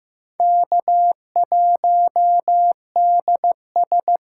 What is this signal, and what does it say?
Morse code, heard as a keyed tone, about 15 wpm, 700 Hz. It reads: K1DS
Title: USA Callsign DE Callsign - 4 Characters - 15wpm